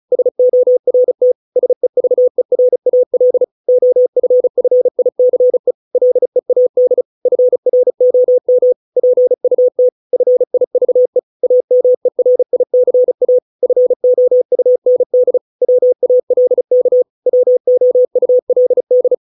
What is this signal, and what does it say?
Morse code, heard as a keyed tone, about 34 wpm, 500 Hz. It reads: SORT SEVERAL OFFICE LEAD FROM PUT FIVE AMERICA FOUND WALK WOULD